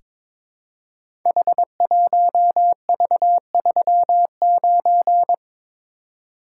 Morse code, heard as a keyed tone, about 23 wpm, 700 Hz. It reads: H1V39
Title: Code Groups - 5 Letter:Numbers - 22wpm